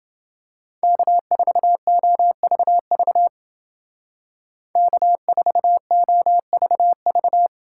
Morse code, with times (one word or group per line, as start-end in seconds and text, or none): 0.8-3.3 K4OVV
4.8-7.5 K4OVV